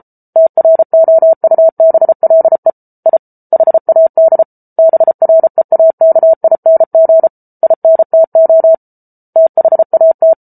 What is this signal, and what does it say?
Morse code, heard as a keyed tone, about 34 wpm, 650 Hz. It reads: TROUBLE I HAD BREAKING INTO THAT